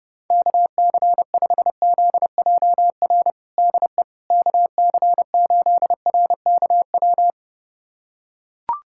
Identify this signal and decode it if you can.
Morse code, heard as a keyed tone, about 30 wpm, 700 Hz. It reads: KC5ZJR DE KC8RKW
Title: USA Callsign DE Callsign - 6 Characters - 30wpm